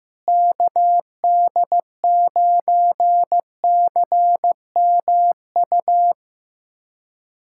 Morse code, heard as a keyed tone, about 15 wpm, 700 Hz. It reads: KD9CMU
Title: USA Callsign DE Callsign - 6 Characters - 15wpm